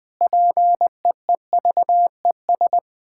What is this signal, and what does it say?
Morse code, heard as a keyed tone, about 20 wpm, 700 Hz. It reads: PEEVES
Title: Pangram Challenge v2 - 20wpm